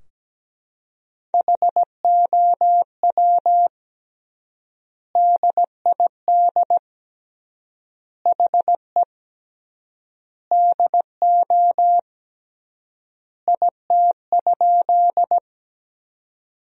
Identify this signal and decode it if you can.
Morse code, heard as a keyed tone, about 17 wpm, 700 Hz. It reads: HOW DID HE DO IT?
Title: 3x Word-Spacing - Sentences from Top 100 Words - 17wpm